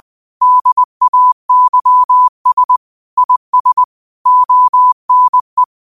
Morse code, heard as a keyed tone, about 20 wpm, 1000 Hz. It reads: DAYS IS ONE
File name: 20wpm